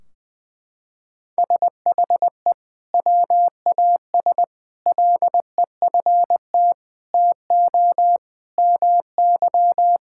Morse code, heard as a keyed tone, about 20 wpm, 700 Hz. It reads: SHE WAS LEFT TO MY
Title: Sentences from Top 1600 Words 20wpm